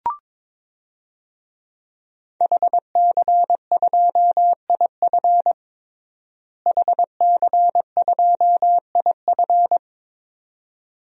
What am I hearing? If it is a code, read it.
Morse code, heard as a keyed tone, about 22 wpm, 700 Hz. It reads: HC2IF HC2IF
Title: DXCC-100 Call Signs - 5 Characters - Repeated 2x - 22wpm